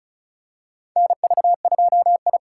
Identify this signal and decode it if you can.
Morse code, heard as a keyed tone, about 34 wpm, 700 Hz. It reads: NV2I
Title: USA Call Signs - 4 Characters - 4x Character Spacing - 35wpm